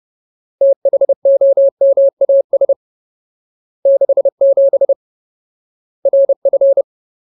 Morse code, heard as a keyed tone, about 30 wpm, 550 Hz. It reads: THOMAS 67 RF
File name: Ultimate Word List / 30WPM Ultimate